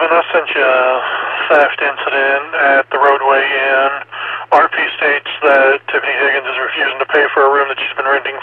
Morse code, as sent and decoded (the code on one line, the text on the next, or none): none